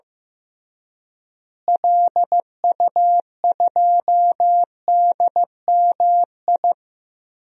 Morse code, heard as a keyed tone, about 15 wpm, 700 Hz. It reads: LU2DMI
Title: DXCC-100 Call Signs - 6 Characters - Repeated 3x - 15wpm